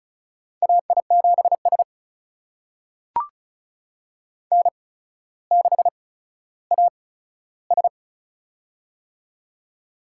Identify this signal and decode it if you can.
Morse code, heard as a keyed tone, about 36 wpm, spaced as for 15 wpm, 700 Hz. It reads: AI7S N6AS
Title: USA Call Signs - 4 Characters - 8x Character Spacing - 35wpm